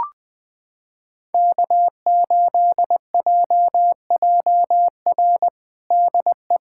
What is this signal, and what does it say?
Morse code, heard as a keyed tone, about 20 wpm, 700 Hz. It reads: K8JJR DE
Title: USA Callsign DE Callsign - 5 Characters - 20wpm